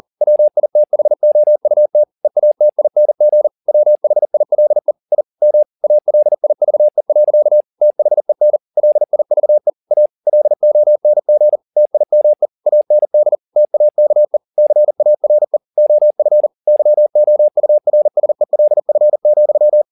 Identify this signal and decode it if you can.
Morse code, heard as a keyed tone, about 40 wpm, 600 Hz. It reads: WITHOUT EATING WHILE I M ALIVE. THEN LIVE A LONG TIME AND TAKE CARE OF YOURSELF,